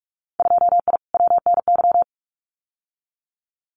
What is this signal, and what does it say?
Morse code, heard as a keyed tone, about 46 wpm, 700 Hz. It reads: 2I WNY
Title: Winter Field Day Exchanges Encore 45wpm